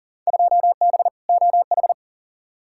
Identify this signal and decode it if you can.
Morse code, heard as a keyed tone, about 40 wpm, 700 Hz. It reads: 2B OH
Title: Field Day Exchanges - 40wpm